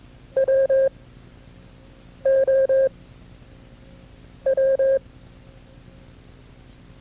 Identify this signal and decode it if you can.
Morse code, heard as a keyed tone, about 22 wpm, 540 Hz. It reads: W O W
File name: FreeFiles / CW-practice / KB6MT-tape2-side1